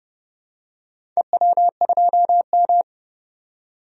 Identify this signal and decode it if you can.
Morse code, heard as a keyed tone, about 30 wpm, 700 Hz. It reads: EW2M
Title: DXCC-100 Call Signs - 4 Characters - 30wpm